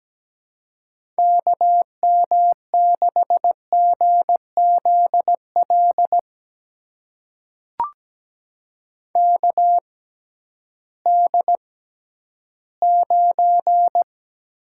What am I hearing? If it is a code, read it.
Morse code, heard as a keyed tone, about 17 wpm, spaced as for 11 wpm, 700 Hz. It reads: KM6GZL KD9
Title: USA Call Signs - 6 Characters - 6x Character Spacing - 17wpm